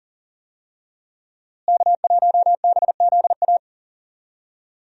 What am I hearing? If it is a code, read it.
Morse code, heard as a keyed tone, about 40 wpm, 700 Hz. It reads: K1BZA
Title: USA Call Signs - 5 Characters - Repeated 3x - 40wpm